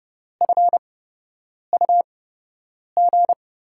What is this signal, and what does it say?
Morse code, heard as a keyed tone, about 30 wpm, spaced as for 8 wpm, 700 Hz. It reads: FUG
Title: USA Call Signs - 5 Characters - 8x Character Spacing - 30wpm